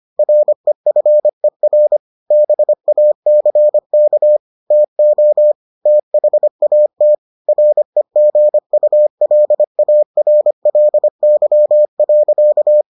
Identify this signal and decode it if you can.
Morse code, heard as a keyed tone, about 25 wpm, 600 Hz. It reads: REFER BACK TO THAT REGULARLY.